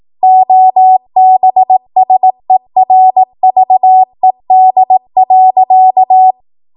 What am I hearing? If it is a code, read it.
Morse code, heard as a keyed tone, about 18 wpm, 750 Hz. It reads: OBSERVED.
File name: Morse / 18WPM